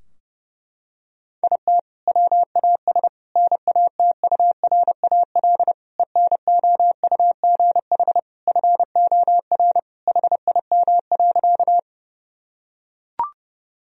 Morse code, heard as a keyed tone, about 32 wpm, 700 Hz. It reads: IT WAS NATURAL ENOUGH FOR HIM.
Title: Sentences from Top 1000 Words - Review Part 4 - 30wpm